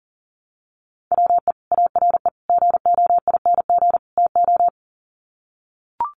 Morse code, heard as a keyed tone, about 40 wpm, 700 Hz. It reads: WE ARE GOING TO
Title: Sets of 4 Words 40wpm